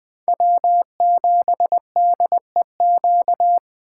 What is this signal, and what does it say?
Morse code, heard as a keyed tone, about 20 wpm, 700 Hz. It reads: W7DEQ
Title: USA Call Signs - 5 Characters - 3x Character Spacing - 20wpm